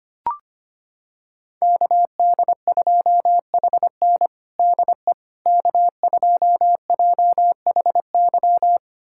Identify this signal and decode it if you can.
Morse code, heard as a keyed tone, about 25 wpm, 700 Hz. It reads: KD2HN DE K2JHY
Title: USA Callsign DE Callsign - 5 Characters - 25wpm